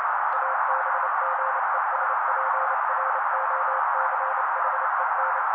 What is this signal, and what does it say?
Morse code, heard as a keyed tone, about 29 wpm, 530 Hz. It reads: ABGESPROCHEN
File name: daily / 27wpm noise